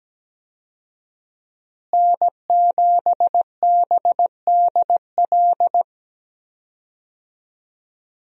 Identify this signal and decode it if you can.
Morse code, heard as a keyed tone, about 17 wpm, 700 Hz. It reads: N7BDL
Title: USA Call Signs - 5 Characters - Repeated 3x - 17wpm